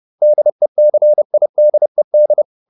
Morse code, heard as a keyed tone, about 30 wpm, 600 Hz. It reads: DECIDED